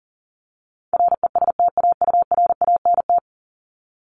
Morse code, heard as a keyed tone, about 40 wpm, 700 Hz. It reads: RESTAURANT
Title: Top 1751-2000 Words 40wpm